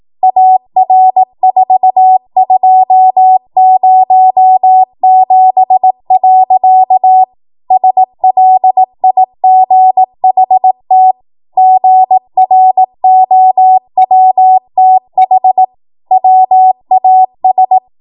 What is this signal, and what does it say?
Morse code, heard as a keyed tone, about 18 wpm, 750 Hz. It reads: AR4207. SLIGHT GROWTH WAS
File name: Morse / 18WPM